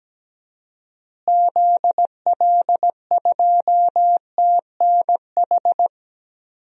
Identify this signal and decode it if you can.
Morse code, heard as a keyed tone, about 17 wpm, 700 Hz. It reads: ZL2TNH
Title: DXCC-100 Call Signs - 6 Characters - Repeated 2x - 17wpm